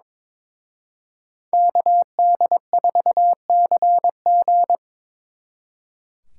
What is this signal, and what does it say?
Morse code, heard as a keyed tone, about 27 wpm, 700 Hz. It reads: KD4CG
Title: Speed Racing - USA 2x2 Call Signs 22wpm